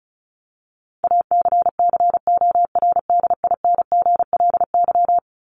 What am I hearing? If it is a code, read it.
Morse code, heard as a keyed tone, about 34 wpm, 700 Hz. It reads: ACCORDINGLY
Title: LY Suffix - Top 100 Words with Base Words 35wpm